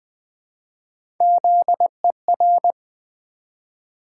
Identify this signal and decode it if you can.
Morse code, heard as a keyed tone, about 20 wpm, 700 Hz. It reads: ZER
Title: Build A Word - 1 Letter At A Time - 4-Letter Words - 20wpm